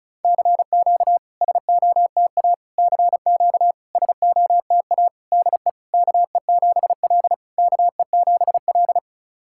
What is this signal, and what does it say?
Morse code, heard as a keyed tone, about 34 wpm, 690 Hz. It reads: CQ SOTA CQ SOTA DE KE7L KE7L
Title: SOTA QSO Practice 35wpm